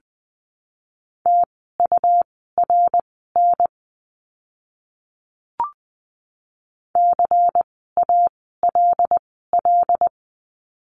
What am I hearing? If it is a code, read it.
Morse code, heard as a keyed tone, about 20 wpm, spaced as for 15 wpm, 700 Hz. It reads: TURN CALL
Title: 206 Lesson 33f - Letter j - Words 20wpm